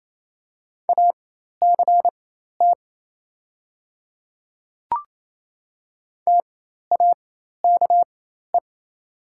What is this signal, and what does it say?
Morse code, heard as a keyed tone, about 27 wpm, spaced as for 13 wpm, 700 Hz. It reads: ACT TAKE
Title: 4x Character Spacing - Top 200 Words - 28wpm